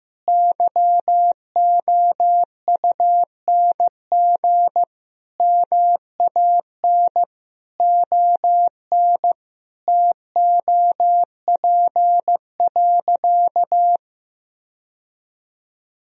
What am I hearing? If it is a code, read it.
Morse code, heard as a keyed tone, about 15 wpm, 700 Hz. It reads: YOUNG MAN ON TOP.
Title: Sentences from Top 1000 Words - Review Part 2 - 15wpm